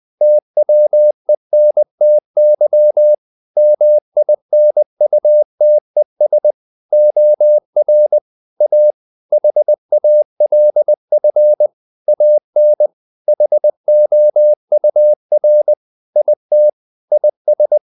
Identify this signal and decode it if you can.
Morse code, heard as a keyed tone, about 20 wpm, 600 Hz. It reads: TWENTY MINUTES OR A HALF AN HOUR IT IS
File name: the old man and the sea part 2 20wpm